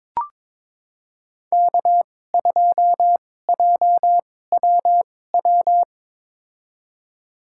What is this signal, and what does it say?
Morse code, heard as a keyed tone, about 22 wpm, spaced as for 16 wpm, 700 Hz. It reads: K2JWW
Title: USA Call Signs - 5 Characters - 2x Character Spacing - 22wpm